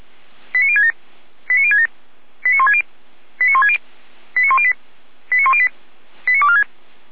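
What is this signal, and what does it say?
Five-tone selective call (ZVEI), heard as quick runs of five tones, tones of about 70 ms. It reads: 89067 89067 89160 89160 89108 89108 89246
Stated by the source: ZVEI five-tone alert received by SDR, from feuerwehr-orlamuende.org fire brigade (Orlamünde).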